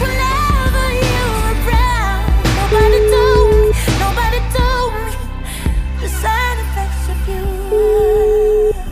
Ringback tone (Europe, 425 Hz), with noise.